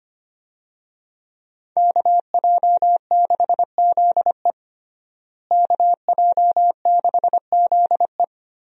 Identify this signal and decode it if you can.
Morse code, heard as a keyed tone, about 25 wpm, 700 Hz. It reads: KJ6ZE KJ6ZE
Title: USA Call Signs - 5 Characters - Repeated 3x - 25wpm